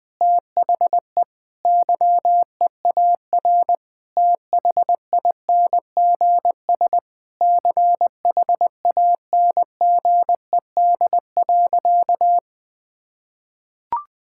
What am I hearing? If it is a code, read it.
Morse code, heard as a keyed tone, about 20 wpm, 700 Hz. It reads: THE YEAR THINGS CHANGED.
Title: Sentences with 5-Word Phrases - Spoken Beginning - 20wpm